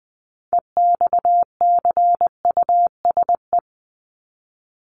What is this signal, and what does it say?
Morse code, heard as a keyed tone, about 20 wpm, 700 Hz. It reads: EXCUSE
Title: Top 501-750 Words 20wpm